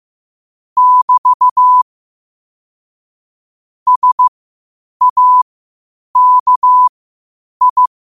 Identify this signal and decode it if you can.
Morse code, heard as a keyed tone, about 15 wpm, spaced as for 8 wpm, 1000 Hz. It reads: <BT> SAKI
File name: qotdfiles / 8wpm